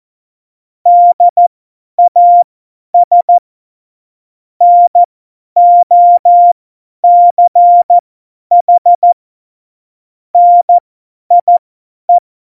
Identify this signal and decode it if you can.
Morse code, heard as a keyed tone, about 14 wpm, spaced as for 10 wpm, 700 Hz. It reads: DAS NOCH NIE
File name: Morsekurs / Witze / Witz 12 Mai 2020, 50 F70